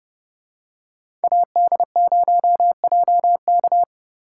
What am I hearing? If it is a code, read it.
Morse code, heard as a keyed tone, about 30 wpm, 700 Hz. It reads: AD0JK